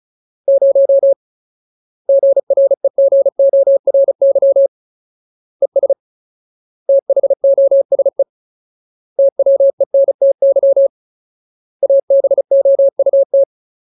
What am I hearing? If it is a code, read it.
Morse code, heard as a keyed tone, about 34 wpm, 550 Hz. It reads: 0 GREGORY ES THOSE TWENTY ABOUT
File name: Ultimate Word List / 35WPM Ultimate